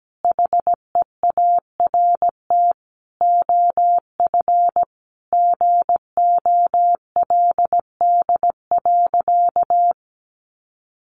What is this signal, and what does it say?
Morse code, heard as a keyed tone, about 17 wpm, 700 Hz. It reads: HEART OF GOLD.
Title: Sentences from Top 1200 Words 17wpm